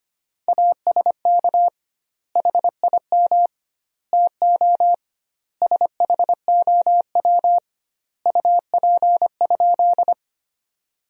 Morse code, heard as a keyed tone, about 25 wpm, 700 Hz. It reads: ASK HIM TO SHOW UP?
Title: 2x Word-Spacing - Sentences from Top 200 Words - 25wpm